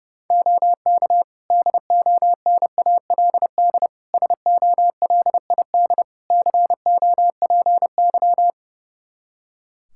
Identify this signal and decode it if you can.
Morse code, heard as a keyed tone, about 30 wpm, 700 Hz. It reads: OK DONALD SOLID COPY
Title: Practice QSO - Exchange 3 - More Info - 30wpm